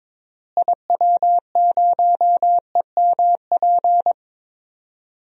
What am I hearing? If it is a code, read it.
Morse code, heard as a keyed tone, about 22 wpm, 700 Hz. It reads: IW0EMP
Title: DXCC-100 Call Signs - 6 Characters - Repeated 3x - 22wpm